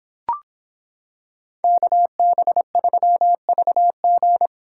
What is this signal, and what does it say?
Morse code, heard as a keyed tone, about 26 wpm, 700 Hz. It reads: KB3VG